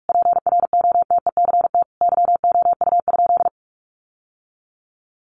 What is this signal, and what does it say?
Morse code, heard as a keyed tone, about 44 wpm, 700 Hz. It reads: PROTECT YOU?
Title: Sentences from Top 900 Words 45wpm